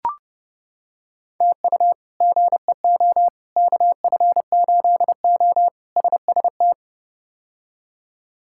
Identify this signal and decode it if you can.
Morse code, heard as a keyed tone, about 30 wpm, 700 Hz. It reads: TU GEO KF8O SST
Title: K1USN SST QSO Practice - 30wpm